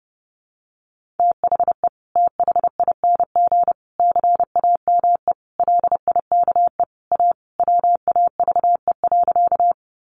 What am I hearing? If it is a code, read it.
Morse code, heard as a keyed tone, about 30 wpm, 700 Hz. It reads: THE THING CAME LIKE A WAVE.
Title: Sentences from Top 1400 Words 30wpm